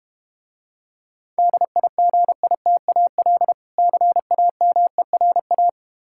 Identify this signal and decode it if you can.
Morse code, heard as a keyed tone, about 32 wpm, 700 Hz. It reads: DIGITAL CAMERA